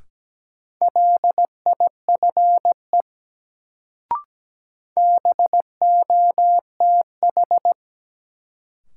Morse code, heard as a keyed tone, about 17 wpm, 700 Hz. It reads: LIFE BOTH
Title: Sentences from Top 300 Words - New Word Drill - 17wpm